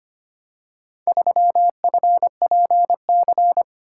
Morse code, heard as a keyed tone, about 25 wpm, 700 Hz. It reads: 3FPC
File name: practice / 25wpm / Four Letter-Number Combinations - 25wpm